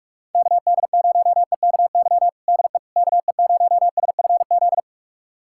Morse code, heard as a keyed tone, about 46 wpm, 700 Hz. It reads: KD0EKY DE KE0SFZ
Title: USA Callsign DE Callsign - 6 Characters - 45wpm